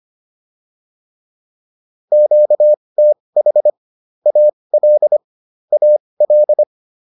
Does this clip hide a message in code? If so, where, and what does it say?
Yes: Morse code from 2.1 to 6.6 s, heard as a keyed tone, about 25 wpm, spaced as for 20 wpm, 600 Hz. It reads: QTH AL AL